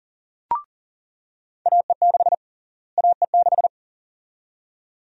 Morse code, heard as a keyed tone, about 40 wpm, 700 Hz. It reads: AE6 AE6
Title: USA Call Sign Prefixes - Repeated 2x - 40wpm